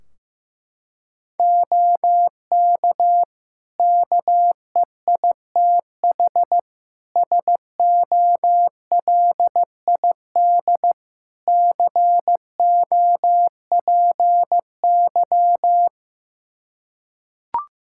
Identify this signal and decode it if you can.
Morse code, heard as a keyed tone, about 15 wpm, 700 Hz. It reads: OK KEITH SOLID COPY
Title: Practice QSO - Exchange 3 - More Info - 15wpm